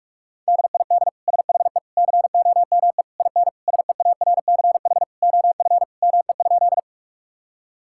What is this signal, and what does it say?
Morse code, heard as a keyed tone, about 41 wpm, 700 Hz. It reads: DID SHE COME IN SEARCH OF ME?
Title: Sentences from Top 1000 Words - Review Part 5 - 45wpm